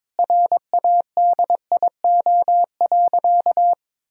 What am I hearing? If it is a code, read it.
Morse code, heard as a keyed tone, about 22 wpm, 700 Hz. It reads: RADIO.
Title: Sentences from Top 1000 Words - Review Part 3 - 22wpm